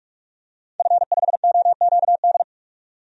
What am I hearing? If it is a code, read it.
Morse code, heard as a keyed tone, about 46 wpm, 700 Hz. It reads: F5OQD
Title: DXCC-100 Call Signs - 5 Characters - 45wpm